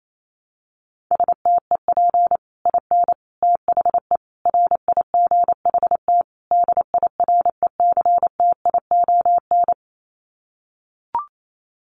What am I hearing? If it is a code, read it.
Morse code, heard as a keyed tone, about 28 wpm, 700 Hz. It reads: STEP IN THE RIGHT DIRECTION
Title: Sets of 5 Words Encore 28wpm